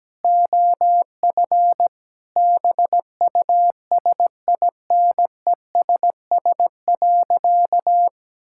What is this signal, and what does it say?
Morse code, heard as a keyed tone, about 17 wpm, 700 Hz. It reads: OF BUSINESS.
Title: Sentences from Top 1000 Words - Review Part 5 - 17wpm